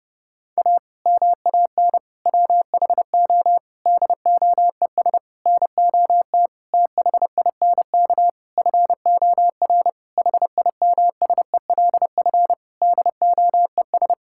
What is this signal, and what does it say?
Morse code, heard as a keyed tone, about 30 wpm, 700 Hz. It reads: A MAN WHO DOES NOT THINK FOR HIMSELF DOES
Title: Sentences from Top 1600 Words 30wpm